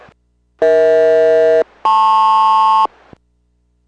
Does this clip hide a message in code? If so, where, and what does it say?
Yes: aircraft SELCAL from 0.6 to 2.9 s, heard as two pairs of tones. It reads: EH-LP